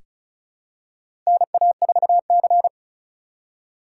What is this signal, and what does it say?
Morse code, heard as a keyed tone, about 36 wpm, 700 Hz. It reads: NA4C
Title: K1USN SST QSO Practice - 35wpm